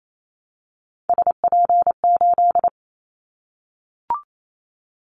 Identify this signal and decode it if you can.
Morse code, heard as a keyed tone, about 28 wpm, 700 Hz. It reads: SP8
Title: Callsign Prefixes for Top 25 DXCC - Repeated 3x - 28wpm